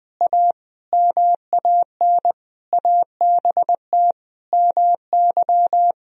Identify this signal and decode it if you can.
Morse code, heard as a keyed tone, about 20 wpm, 700 Hz. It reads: A MAN ABT MY
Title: Sentences from Top 400 Words and Abbreviations 20wpm